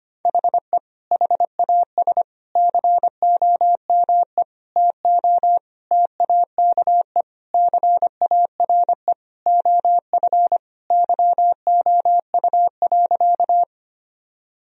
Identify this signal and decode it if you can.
Morse code, heard as a keyed tone, about 25 wpm, 700 Hz. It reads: HE HAS COME TO TAKE CARE OF YOU.
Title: Sentences from Top 1000 Words - Review Part 1 - 25wpm